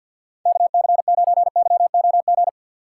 Morse code, heard as a keyed tone, about 53 wpm, 700 Hz. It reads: KC9YOG
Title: USA Call Signs - 6 Characters - 6x Character Spacing - 50wpm